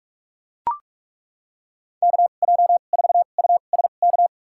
Morse code, heard as a keyed tone, about 41 wpm, spaced as for 31 wpm, 700 Hz. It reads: KJ4USK
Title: USA Call Signs - 6 Characters - 2x Character Spacing - 45wpm